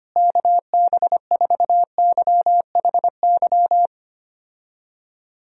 Morse code, heard as a keyed tone, about 25 wpm, 700 Hz. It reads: KB4YHY
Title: USA Call Signs - 6 Characters - Repeated 3x - 25wpm